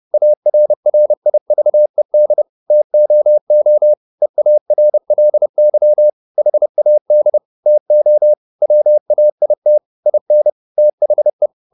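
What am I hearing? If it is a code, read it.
Morse code, heard as a keyed tone, about 30 wpm, 600 Hz. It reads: ARRIVED TOO EARLY HAD TO WAIT IN THE
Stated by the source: Morse code at 29 wpm, sending Ken Follett's novel Never in English, from CW med Gnister.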